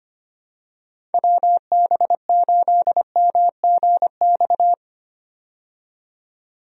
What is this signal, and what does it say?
Morse code, heard as a keyed tone, about 25 wpm, 700 Hz. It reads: WB8MGX
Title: USA Call Signs - 6 Characters - 25wpm